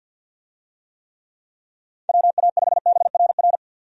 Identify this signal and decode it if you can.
Morse code, heard as a keyed tone, about 51 wpm, 700 Hz. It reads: WA5BRR
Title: USA Call Signs - 6 Characters - 50wpm